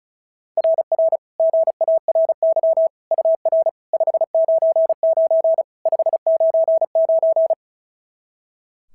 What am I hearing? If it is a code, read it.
Morse code, heard as a keyed tone, about 36 wpm, 690 Hz. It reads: RR GARY UR 599 599
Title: SOTA QSO Practice 35wpm